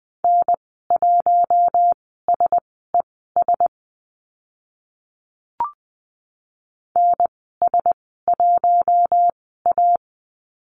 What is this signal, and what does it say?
Morse code, heard as a keyed tone, about 20 wpm, spaced as for 14 wpm, 700 Hz. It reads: N1SES NS1A
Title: 042 Lesson 8g - Number 1 - Call Signs 20wpm